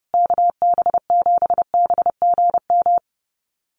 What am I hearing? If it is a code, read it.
Morse code, heard as a keyed tone, about 28 wpm, 700 Hz. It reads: KB7BGM
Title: USA Call Signs Repeated 3x 30wpm